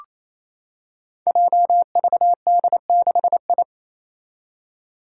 Morse code, heard as a keyed tone, about 28 wpm, 700 Hz. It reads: JVD6I
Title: Five Challenging Letter-Number Combinations 28wpm